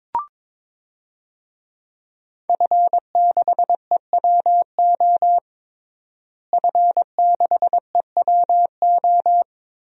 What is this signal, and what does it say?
Morse code, heard as a keyed tone, about 22 wpm, 700 Hz. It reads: F6EWO F6EWO
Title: DXCC-100 Call Signs - 5 Characters - Repeated 3x - 22wpm